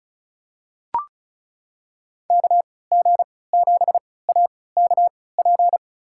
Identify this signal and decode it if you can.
Morse code, heard as a keyed tone, about 36 wpm, spaced as for 20 wpm, 700 Hz. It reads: KG7AKP
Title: USA Call Signs - 3x Character Spacing - 35wpm